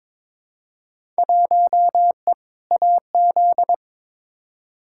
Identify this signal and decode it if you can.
Morse code, heard as a keyed tone, about 22 wpm, 700 Hz. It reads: 1E AZ
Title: Field Day Exchanges - 22wpm